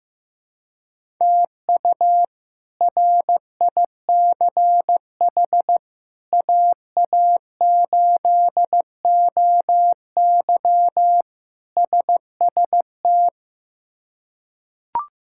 Morse code, heard as a keyed tone, about 15 wpm, 700 Hz. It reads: TU RICH AA8OY SST
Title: K1USN SST QSO Practice - 15wpm